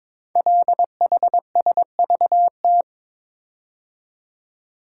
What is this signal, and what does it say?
Morse code, heard as a keyed tone, about 22 wpm, 700 Hz. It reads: LHSVT
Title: Code Groups - 5 Letters - 22wpm